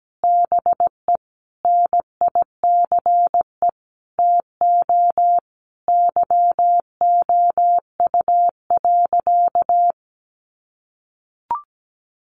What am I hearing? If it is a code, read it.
Morse code, heard as a keyed tone, about 17 wpm, 700 Hz. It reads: BE NICE TO YOU.
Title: Sentences from Top 1100 Words 17wpm